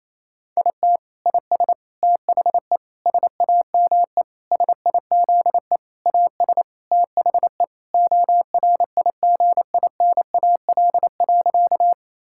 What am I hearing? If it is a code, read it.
Morse code, heard as a keyed tone, about 28 wpm, 700 Hz. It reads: IT IS THE SAME SIZE AS THE ORIGINAL.
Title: Sentences from Top 1000 Words - Review Part 6 - 28wpm